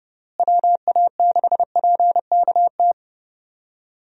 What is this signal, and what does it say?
Morse code, heard as a keyed tone, about 31 wpm, 700 Hz. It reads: WA6PKT
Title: USA Call Signs - 6 Characters - Repeated 2x - 30wpm